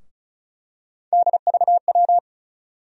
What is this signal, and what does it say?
Morse code, heard as a keyed tone, about 36 wpm, 700 Hz. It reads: DVW
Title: USA Call Sign Postfixes - 3 Letters 35wpm